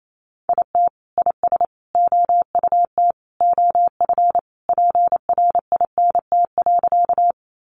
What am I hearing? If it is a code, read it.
Morse code, heard as a keyed tone, about 28 wpm, 700 Hz. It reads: IT IS OUT OF PRINT.
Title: Sentences from Top 1400 Words 28wpm